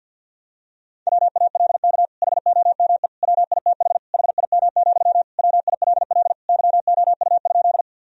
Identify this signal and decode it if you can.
Morse code, heard as a keyed tone, about 47 wpm, 700 Hz. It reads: WALK HOME WITH HIM, WILL YOU?
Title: Sentences from Top 1000 Words - Review Part 1 - 50wpm